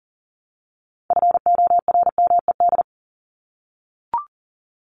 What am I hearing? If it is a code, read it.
Morse code, heard as a keyed tone, about 40 wpm, 700 Hz. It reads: FORMED
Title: ED Suffix - Top 100 Words with Base Words 40wpm